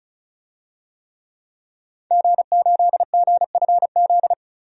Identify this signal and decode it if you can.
Morse code, heard as a keyed tone, about 34 wpm, 700 Hz. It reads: G8GFZ
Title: DXCC-100 Call Signs - 5 Characters - 35wpm